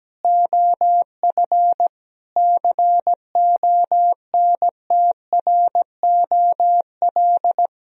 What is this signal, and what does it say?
Morse code, heard as a keyed tone, about 17 wpm, 700 Hz. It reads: OF CONTROL
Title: Sentences from Top 1000 Words - Review Part 3 - 17wpm